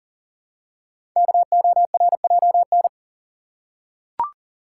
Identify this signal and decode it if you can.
Morse code, heard as a keyed tone, about 40 wpm, 690 Hz. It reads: KORJN
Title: Code Groups - 5 Letters - 40wpm